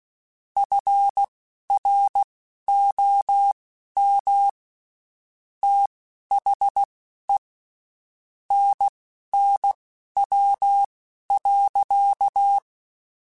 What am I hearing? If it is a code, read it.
Morse code, heard as a keyed tone, about 16 wpm, spaced as for 12 wpm, 800 Hz. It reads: FROM THE NNW.
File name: Spaced Characters / Spaced Characters-mp3s / Weather 5 12 sp @15